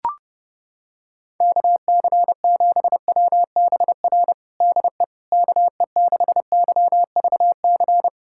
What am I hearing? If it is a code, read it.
Morse code, heard as a keyed tone, about 30 wpm, 700 Hz. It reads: KC7WBR DE KE6YVC
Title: USA Callsign DE Callsign - 6 Characters - 30wpm